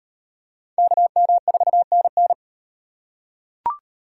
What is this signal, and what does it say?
Morse code, heard as a keyed tone, about 38 wpm, 700 Hz. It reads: KM4NN